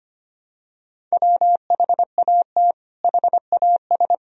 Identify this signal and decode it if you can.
Morse code, heard as a keyed tone, about 25 wpm, 700 Hz. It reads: WHAT HAS
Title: Sentences from Top 500 Words and Abbreviations 25wpm